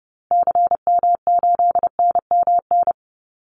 Morse code, heard as a keyed tone, about 30 wpm, 700 Hz. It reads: CM8NMN
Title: Call Signs Repeated 2x - 30wpm